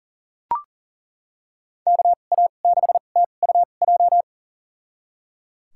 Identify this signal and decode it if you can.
Morse code, heard as a keyed tone, about 39 wpm, spaced as for 29 wpm, 700 Hz. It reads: KA6TUJ
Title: USA Call Signs - 6 Characters - 2x Character Spacing - 40wpm